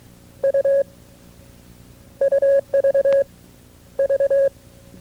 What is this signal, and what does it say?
Morse code, heard as a keyed tone, about 23 wpm, spaced as for 13 wpm, 550 Hz. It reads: U UVV
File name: FreeFiles / CW-practice / KB6MT-tape2-side2